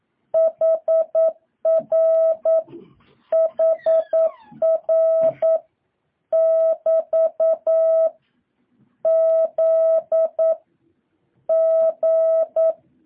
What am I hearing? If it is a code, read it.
Morse code, heard as a keyed tone, about 9 wpm, 650 Hz. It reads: HR HR <BT> Z G
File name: jh8cbh / 2023cw